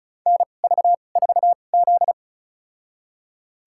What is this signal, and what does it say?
Morse code, heard as a keyed tone, about 34 wpm, spaced as for 25 wpm, 700 Hz. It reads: NV4Z
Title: USA Call Signs - 4 Characters - 2x Character Spacing - 35wpm